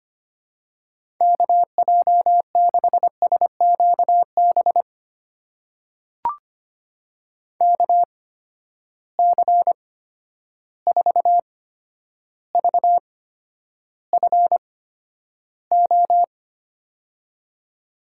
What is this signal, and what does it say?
Morse code, heard as a keyed tone, about 25 wpm, spaced as for 11 wpm, 700 Hz. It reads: KJ6SQB KC4VFO
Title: USA Call Signs - 6 Characters - 8x Character Spacing - 25wpm